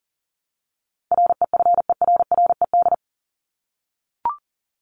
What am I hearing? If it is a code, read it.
Morse code, heard as a keyed tone, about 40 wpm, 700 Hz. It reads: REFERRED
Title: ED Suffix - Top 100 Words with Base Words 40wpm